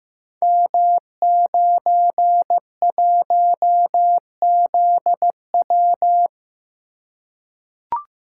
Morse code, heard as a keyed tone, about 15 wpm, 700 Hz. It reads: M91ZW
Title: Code Groups - 5 Letter:Numbers - 15wpm